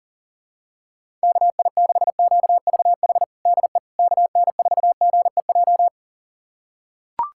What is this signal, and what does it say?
Morse code, heard as a keyed tone, about 37 wpm, 700 Hz. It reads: KI6QVH DE KN4GEJ
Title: USA Callsign DE Callsign 40wpm